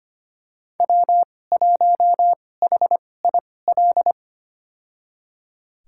Morse code, heard as a keyed tone, about 25 wpm, spaced as for 18 wpm, 700 Hz. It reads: W1HIL
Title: USA Call Signs - 5 Characters - 2x Character Spacing - 25wpm